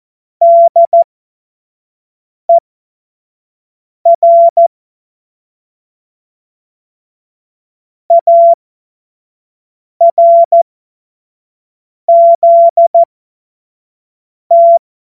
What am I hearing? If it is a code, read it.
Morse code, heard as a keyed tone, about 14 wpm, spaced as for 5 wpm, 680 Hz. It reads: DER ARZT
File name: Morsekurs / Witze / Witz 27. Januar 2026 25 F70